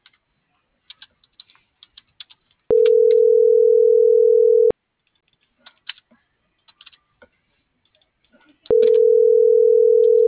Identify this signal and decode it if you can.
Ringback tone (North America), 440 + 480 Hz.